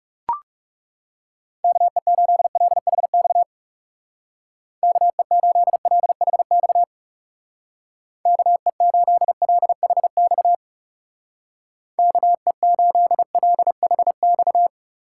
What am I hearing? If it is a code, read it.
Morse code, heard as a keyed tone, about 35 wpm, 700 Hz. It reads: KE8LHX KE8LHX KE8LHX KE8LHX
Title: Speed Racing - USA Call Signs 30wpm